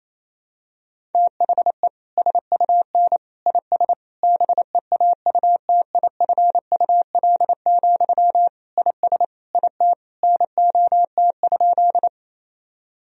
Morse code, heard as a keyed tone, about 28 wpm, 700 Hz. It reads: THE SUN IS BEAUTIFUL, IS IT NOT?
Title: Sentences from Top 1000 Words - Review Part 1 - 28wpm